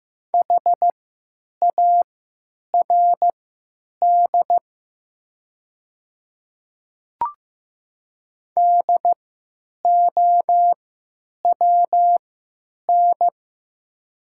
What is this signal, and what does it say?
Morse code, heard as a keyed tone, about 15 wpm, spaced as for 9 wpm, 700 Hz. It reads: HARD DOWN
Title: 3x Character Spacing - Top 300 Words - 15wpm